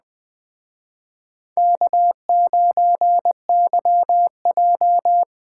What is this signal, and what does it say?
Morse code, heard as a keyed tone, about 20 wpm, 700 Hz. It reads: K9YJ
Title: Speed Racing - USA 1x2 Call Signs 17wpm